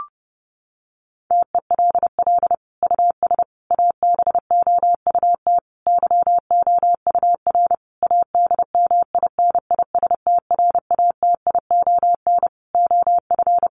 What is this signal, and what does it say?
Morse code, heard as a keyed tone, about 30 wpm, 700 Hz. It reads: TELL US ABOUT YOUR ADMINISTRATION OF
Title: Sentences from Top 1300 Words 30wpm